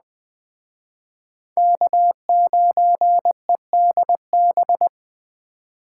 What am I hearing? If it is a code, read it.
Morse code, heard as a keyed tone, about 20 wpm, 700 Hz. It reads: K9EDB